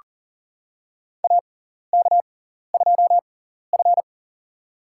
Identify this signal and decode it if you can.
Morse code, heard as a keyed tone, about 40 wpm, spaced as for 14 wpm, 700 Hz. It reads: AK2F
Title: USA Call Signs - 4 Characters - 6x Character Spacing - 40wpm